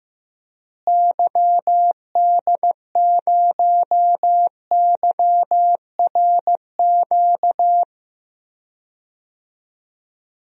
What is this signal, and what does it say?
Morse code, heard as a keyed tone, about 15 wpm, 700 Hz. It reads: YD0YRQ
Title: DXCC-100 Call Signs - 6 Characters - Repeated 3x - 15wpm